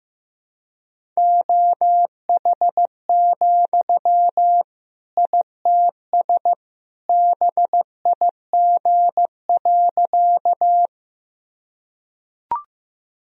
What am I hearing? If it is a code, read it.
Morse code, heard as a keyed tone, about 15 wpm, 700 Hz. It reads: OH, ITS BIG.
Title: Sentences from Top 1000 Words - Review Part 1 - 15wpm